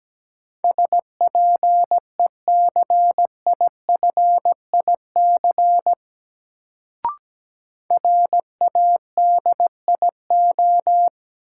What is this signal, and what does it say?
Morse code, heard as a keyed tone, about 17 wpm, 700 Hz. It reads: SPECIFIC RADIO
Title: Sentences from Top 1000 Words - Review Part 3 - New Word Drill - 17wpm